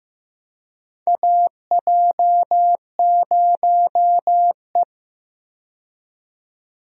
Morse code, heard as a keyed tone, about 15 wpm, 700 Hz. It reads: AJ0E
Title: USA Call Signs - 4 Characters - 15wpm